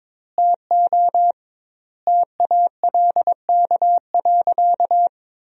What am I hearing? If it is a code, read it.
Morse code, heard as a keyed tone, about 21 wpm, 700 Hz. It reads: TO TALK.
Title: 2x Word-Spacing - Sentences from Top 400 Words - 22wpm